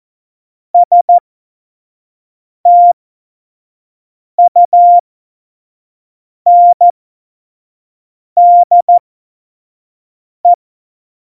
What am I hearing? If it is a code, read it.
Morse code, heard as a keyed tone, about 14 wpm, spaced as for 5 wpm, 700 Hz. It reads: STUNDE
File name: Morsekurs / Witze / Witz 24. September 2019, 25 F70